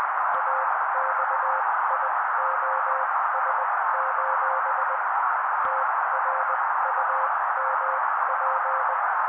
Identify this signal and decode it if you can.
Morse code, heard as a keyed tone, about 20 wpm, spaced as for 16 wpm, 530 Hz. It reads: AXIOS: TRUMP
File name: daily / 16wpm noise